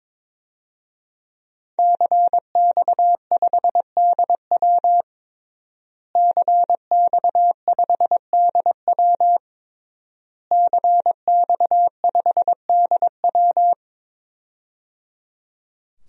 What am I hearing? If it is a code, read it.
Morse code, heard as a keyed tone, about 22 wpm, 700 Hz. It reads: CX5DW CX5DW CX5DW
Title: DXCC-100 Call Signs - 5 Characters - Repeated 3x - 22wpm